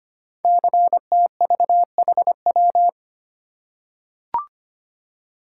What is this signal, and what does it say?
Morse code, heard as a keyed tone, about 25 wpm, 700 Hz. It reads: CTVHW
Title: Code Groups - 5 Letters - 25wpm